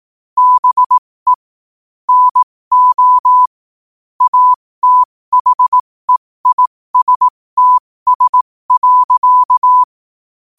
Morse code, heard as a keyed tone, about 18 wpm, spaced as for 15 wpm, 1000 Hz. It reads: BE NO ATHEISTS.
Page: 15 wpm morse code quotes